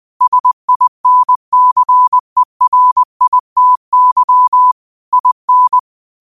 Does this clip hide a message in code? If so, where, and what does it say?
Yes: Morse code from 0.2 to 5.8 s, heard as a keyed tone, about 20 wpm, 1000 Hz. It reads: SINCERITY IN